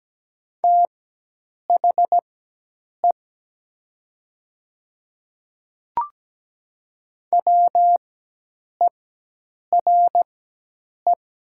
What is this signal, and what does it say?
Morse code, heard as a keyed tone, about 17 wpm, spaced as for 8 wpm, 700 Hz. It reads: THE WERE
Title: 4x Character Spacing - Top 100 Words - 17wpm